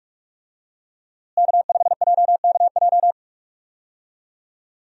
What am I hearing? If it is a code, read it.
Morse code, heard as a keyed tone, about 46 wpm, 700 Hz. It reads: K5JKJ
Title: USA Call Signs - 5 Characters - Repeated 3x - 45wpm